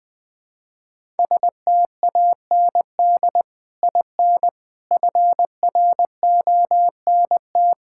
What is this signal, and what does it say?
Morse code, heard as a keyed tone, about 21 wpm, 700 Hz. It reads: STAND IN FRONT
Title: Sentences with 3-Word Phrases - Spoken Beginning - 20wpm